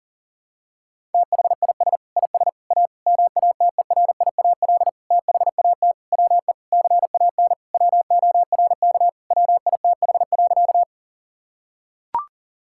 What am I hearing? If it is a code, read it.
Morse code, heard as a keyed tone, about 40 wpm, 700 Hz. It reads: THIS IS A MATERIAL THAT WE CAN WORK WITH.
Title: Sentences from Top 1000 Words - Review Part 5 - 40wpm